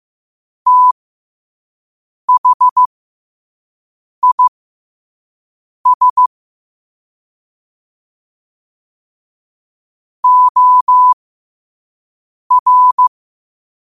Morse code, heard as a keyed tone, about 15 wpm, spaced as for 5 wpm, 1000 Hz. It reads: THIS OR
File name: qotdfiles / 5wpm